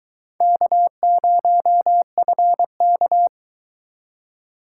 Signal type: Morse code

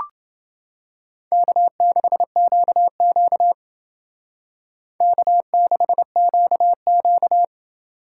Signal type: Morse code

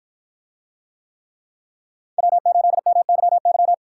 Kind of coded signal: Morse code

